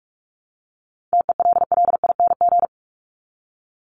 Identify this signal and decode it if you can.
Morse code, heard as a keyed tone, about 44 wpm, 700 Hz. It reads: TELLING
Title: ING Suffix - Top 100 Words with Base Words 45wpm